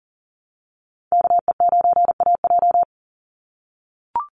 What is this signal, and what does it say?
Morse code, heard as a keyed tone, about 39 wpm, 700 Hz. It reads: KE9AJ
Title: NA SOTA Chaser Callsigns 40wpm